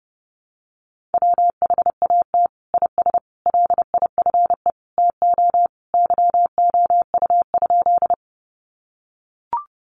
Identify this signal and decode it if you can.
Morse code, heard as a keyed tone, about 28 wpm, 700 Hz. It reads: WHAT IS LIFE TO YOU?
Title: Sentences from Top 300 Words 30wpm